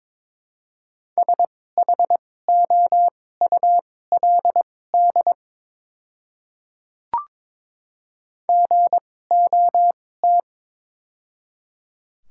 Morse code, heard as a keyed tone, about 22 wpm, spaced as for 16 wpm, 700 Hz. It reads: SHOULD GOT
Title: 2x Character Spacing - Top 300 Words - 22wpm